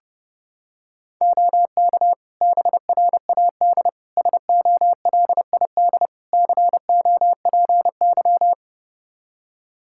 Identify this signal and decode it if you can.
Morse code, heard as a keyed tone, about 30 wpm, 700 Hz. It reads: OK BRAD SOLID COPY
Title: Practice QSO - Exchange 3 - More Info - 30wpm